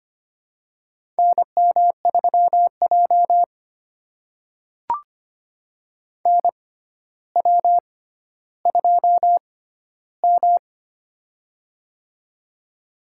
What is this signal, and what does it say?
Morse code, heard as a keyed tone, about 25 wpm, spaced as for 13 wpm, 700 Hz. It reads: NM3J NW2M
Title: USA Call Signs - 4 Characters - 6x Character Spacing - 25wpm